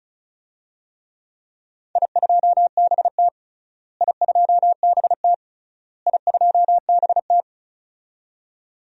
Morse code, heard as a keyed tone, about 34 wpm, 700 Hz. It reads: I2BT I2BT I2BT
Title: DXCC-100 Call Signs - 4 Characters - Repeated 3x - 35wpm